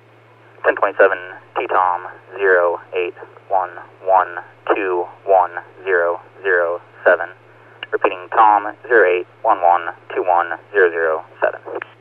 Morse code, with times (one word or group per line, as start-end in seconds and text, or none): none